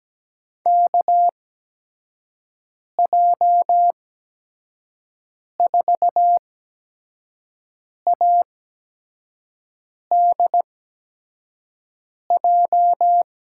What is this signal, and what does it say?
Morse code, heard as a keyed tone, about 17 wpm, spaced as for 5 wpm, 700 Hz. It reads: KJ4ADJ
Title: USA Call Signs - 6 Characters - 8x Character Spacing - 17wpm